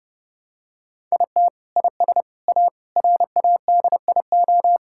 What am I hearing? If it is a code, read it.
Morse code, heard as a keyed tone, about 32 wpm, 700 Hz. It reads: IT IS A RADIO
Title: Sentences from Top 1000 Words - Review Part 5 - 30wpm